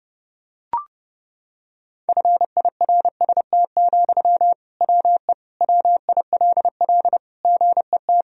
Morse code, heard as a keyed tone, about 30 wpm, 700 Hz. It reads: FIRST, WE WILL GET
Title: Sentences from Top 1000 Words - Review Part 1 - 30wpm